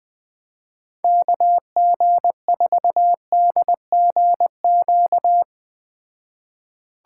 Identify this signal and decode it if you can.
Morse code, heard as a keyed tone, about 20 wpm, 700 Hz. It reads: KG4DGQ